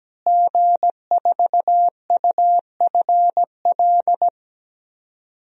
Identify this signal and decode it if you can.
Morse code, heard as a keyed tone, about 17 wpm, 700 Hz. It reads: G4UFL
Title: DXCC-100 Call Signs - 5 Characters - Repeated 3x - 17wpm